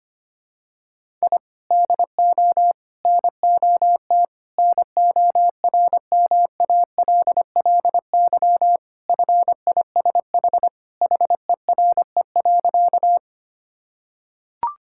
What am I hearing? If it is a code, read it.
Morse code, heard as a keyed tone, about 25 wpm, 700 Hz. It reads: I DO NOT NORMALLY FISH HERE.
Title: Sentences from Top 1000 Words - Review Part 6 - 25wpm